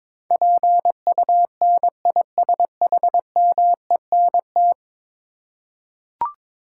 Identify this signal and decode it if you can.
Morse code, heard as a keyed tone, about 23 wpm, 700 Hz. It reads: PUNISHMENT
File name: practice / 22wpm / 10 Letter Words - First 3 Letters Given - 22wpm